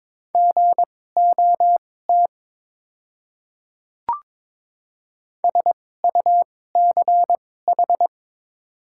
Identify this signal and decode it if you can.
Morse code, heard as a keyed tone, about 22 wpm, spaced as for 16 wpm, 690 Hz. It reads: GOT SUCH
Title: 2x Character Spacing - Top 500 Words - 22wpm